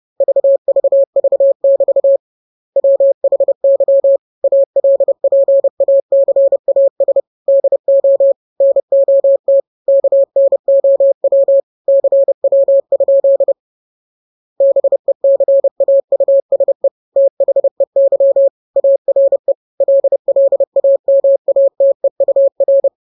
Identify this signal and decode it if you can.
Morse code, heard as a keyed tone, about 30 wpm, 550 Hz. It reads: VVV<BT> WHY ALPACAS DO NOT KNOW CW? BECAUSE THEY ARE LLAMATEUR
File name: jokes / joke 14 30 0000